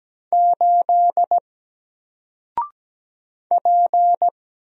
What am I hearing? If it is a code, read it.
Morse code, heard as a keyed tone, about 17 wpm, 700 Hz. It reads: 8 P
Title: Single Letter-Number 17wpm